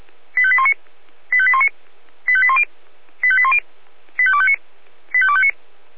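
Five-tone selective call (ZVEI), heard as quick runs of five tones, tones of about 70 ms. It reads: 85819 85819 85810 85810 85259 85259
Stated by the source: ZVEI five-tone selective call received on an SDR.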